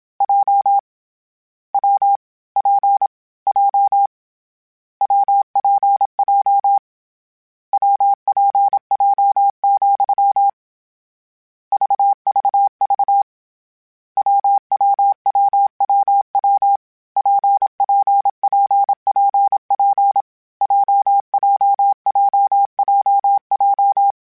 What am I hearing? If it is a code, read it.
Morse code, heard as a keyed tone, about 27 wpm, 800 Hz. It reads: J W P J WPJ WPJ, VVV WWWWW PPPPP JJJJJ
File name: Difficult Combos / Difficult Combos-mp3s / WPJ WPJ1 @25